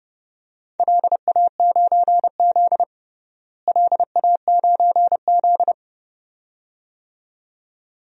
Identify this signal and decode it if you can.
Morse code, heard as a keyed tone, about 31 wpm, 700 Hz. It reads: LA9Z LA9Z
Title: DXCC-100 Call Signs - 4 Characters - Repeated 2x - 30wpm